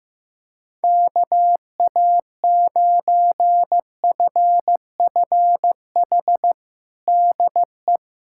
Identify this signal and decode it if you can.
Morse code, heard as a keyed tone, about 15 wpm, 700 Hz. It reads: KA9FFH DE
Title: USA Callsign DE Callsign - 6 Characters - 15wpm